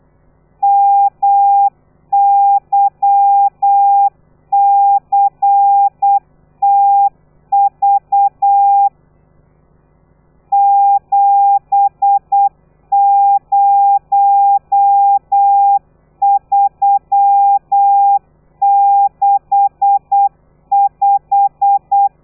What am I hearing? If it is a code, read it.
Morse code, heard as a keyed tone, about 8 wpm, 800 Hz. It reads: MYCTV 70365
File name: telegrafia / i2tfj / Lez13 35